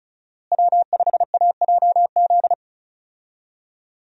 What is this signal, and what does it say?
Morse code, heard as a keyed tone, about 34 wpm, 700 Hz. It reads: W5AJZ
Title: USA Call Signs - 5 Characters - Repeated 3x - 35wpm